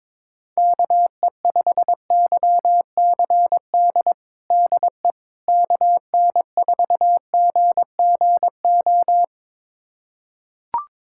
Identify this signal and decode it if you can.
Morse code, heard as a keyed tone, about 21 wpm, 700 Hz. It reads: KE5YCD DE KN4GGO
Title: USA Callsign DE Callsign - 6 Characters - 22wpm